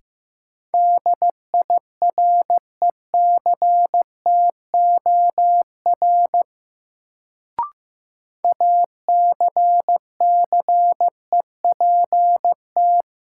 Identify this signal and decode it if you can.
Morse code, heard as a keyed tone, about 15 wpm, 700 Hz. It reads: DIRECTOR ACCEPT
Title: Sentences from Top 1300 Words - New Word Drill Alt - 15wpm